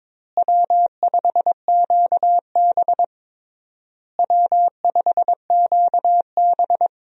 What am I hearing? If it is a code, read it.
Morse code, heard as a keyed tone, about 21 wpm, 700 Hz. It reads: W5QB W5QB
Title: USA Call Signs - 4 Characters - Repeated 3x - 22wpm